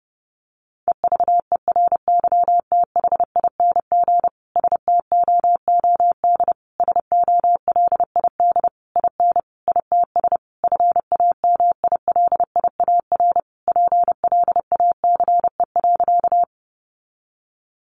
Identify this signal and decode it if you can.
Morse code, heard as a keyed tone, about 30 wpm, 700 Hz. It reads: EVERYTHING STOOD SOLID IN ITS FAMILIAR PLACE.
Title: Sentences from Top 1500 Words 30wpm